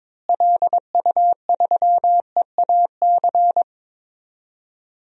Morse code, heard as a keyed tone, about 22 wpm, 700 Hz. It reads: LU3EAC
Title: DXCC-100 Call Signs - 6 Characters - Repeated 2x - 22wpm